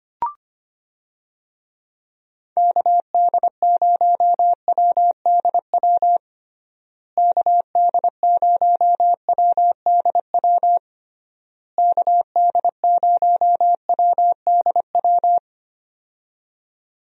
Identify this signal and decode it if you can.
Morse code, heard as a keyed tone, about 25 wpm, 700 Hz. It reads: KD0WDW KD0WDW KD0WDW
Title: USA Call Signs - 6 Characters - Repeated 3x - 25wpm